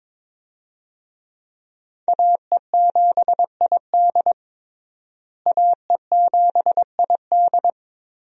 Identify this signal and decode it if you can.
Morse code, heard as a keyed tone, about 22 wpm, 700 Hz. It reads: AE7ID AE7ID
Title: USA Call Signs - 5 Characters - Repeated 3x - 22wpm